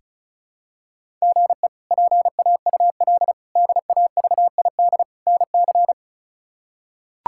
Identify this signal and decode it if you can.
Morse code, heard as a keyed tone, about 36 wpm, 700 Hz. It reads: GE PAUL DAVID NC
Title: K1USN SST QSO Practice - 35wpm